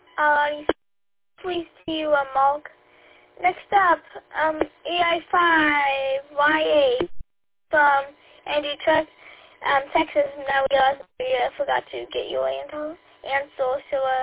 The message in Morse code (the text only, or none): none